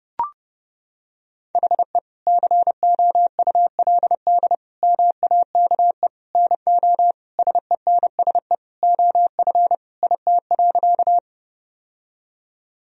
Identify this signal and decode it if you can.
Morse code, heard as a keyed tone, about 30 wpm, 700 Hz. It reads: HE COULD MAKE NO SENSE OF IT.
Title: Sentences from Top 1000 Words - Review Part 4 - 30wpm